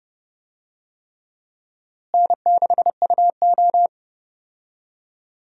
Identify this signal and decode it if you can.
Morse code, heard as a keyed tone, about 30 wpm, 700 Hz. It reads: N6UO
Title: USA Call Signs - 4 Characters - Repeated 3x - 30wpm